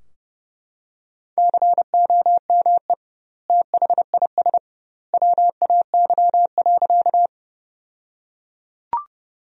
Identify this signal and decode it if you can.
Morse code, heard as a keyed tone, about 31 wpm, 700 Hz. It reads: COME THIS WAY.
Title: 2x Word-Spacing - Sentences from Top 100 Words - 30wpm